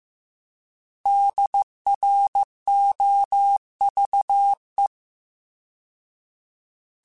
Morse code, heard as a keyed tone, about 15 wpm, 800 Hz. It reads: DROVE